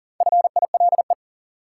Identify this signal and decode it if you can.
Morse code, heard as a keyed tone, about 40 wpm, 700 Hz. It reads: FILE
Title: Sentences from Top 1000 Words - Review Part 4 - New Word Drill - 40wpm